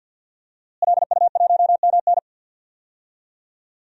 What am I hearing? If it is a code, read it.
Morse code, heard as a keyed tone, about 51 wpm, 700 Hz. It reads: LU1MN
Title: DXCC-100 Call Signs - 5 Characters - Repeated 2x - 50wpm